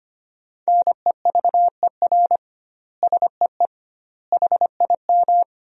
Morse code, heard as a keyed tone, about 25 wpm, 700 Hz. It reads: NEVER SEE HIM
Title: 2x Word-Spacing - Sentences from Top 300 Words - 25wpm